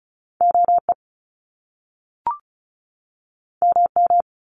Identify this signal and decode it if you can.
Morse code, heard as a keyed tone, about 34 wpm, 700 Hz. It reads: OE MM
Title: Top Two Letter Combinations in Words 35wpm